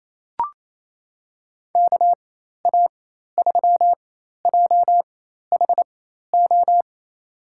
Morse code, heard as a keyed tone, about 27 wpm, spaced as for 13 wpm, 700 Hz. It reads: KA3JHO
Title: USA Call Signs - 4x Character Spacing - 28wpm